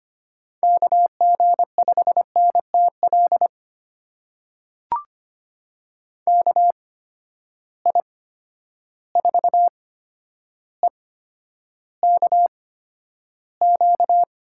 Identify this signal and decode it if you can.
Morse code, heard as a keyed tone, about 25 wpm, spaced as for 11 wpm, 700 Hz. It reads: KG5NTL KI4EKQ
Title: USA Call Signs - 6 Characters - 8x Character Spacing - 25wpm